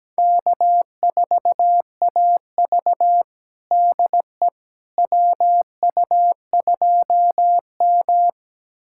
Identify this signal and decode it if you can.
Morse code, heard as a keyed tone, about 17 wpm, 700 Hz. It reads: K4AV DE WU2M
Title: USA Callsign DE Callsign - 4 Characters - 17wpm